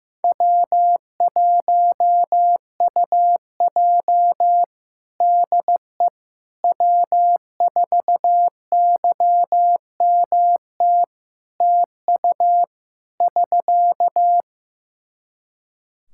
Morse code, heard as a keyed tone, about 15 wpm, 700 Hz. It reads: W1UJ DE W4YMT TU <SK>
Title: Practice QSO - Exchange 4 - Ending - 15wpm